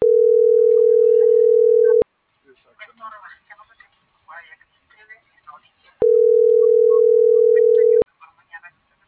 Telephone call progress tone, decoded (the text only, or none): ringback tone